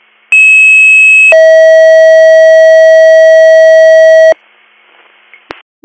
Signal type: two-tone page

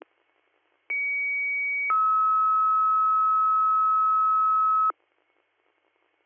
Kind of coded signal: two-tone page

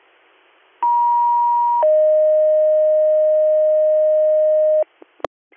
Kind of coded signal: two-tone page

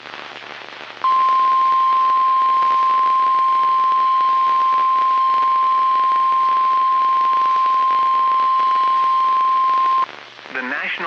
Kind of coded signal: Emergency Alert System (SAME) tones

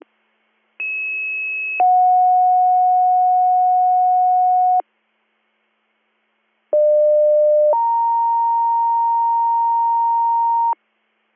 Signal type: two-tone page

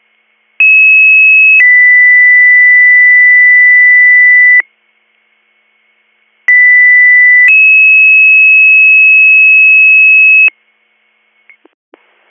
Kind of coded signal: two-tone page

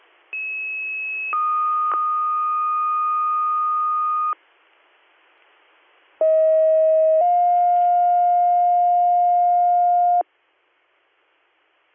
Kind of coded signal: two-tone page

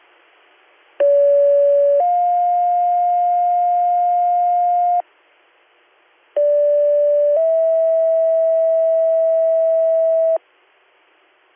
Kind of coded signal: two-tone page

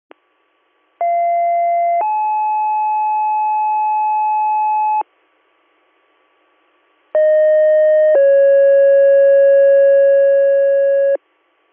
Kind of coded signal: two-tone page

